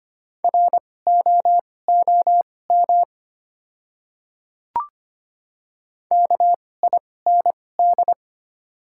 Morse code, heard as a keyed tone, about 25 wpm, spaced as for 18 wpm, 690 Hz. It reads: ROOM KIND
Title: 2x Character Spacing - Top 300 Words - 25wpm